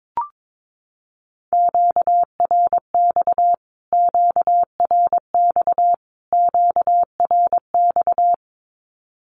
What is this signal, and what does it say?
Morse code, heard as a keyed tone, about 22 wpm, 700 Hz. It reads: QRX QRX QRX
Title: Q-Codes Repeated 3x 22wpm